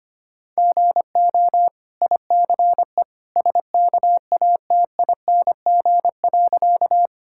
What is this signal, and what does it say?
Morse code, heard as a keyed tone, about 25 wpm, 690 Hz. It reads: GO ICE SKATING.
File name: Easy Sentences with Hints - 25wpm